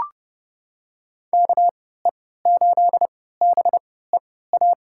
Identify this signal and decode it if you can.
Morse code, heard as a keyed tone, about 32 wpm, spaced as for 17 wpm, 700 Hz. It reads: KE8BEA
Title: USA Call Signs - 6 Characters - 3x Character Spacing - 30wpm